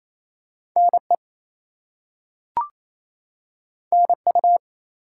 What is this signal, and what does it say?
Morse code, heard as a keyed tone, about 28 wpm, 700 Hz. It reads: NE NU
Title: US States and Canadian Provinces 28wpm